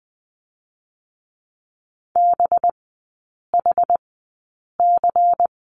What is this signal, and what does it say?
Morse code, heard as a keyed tone, about 20 wpm, spaced as for 8 wpm, 700 Hz. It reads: BHC
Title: 184 Lesson 30e - Letter b - Three Characters 20wpm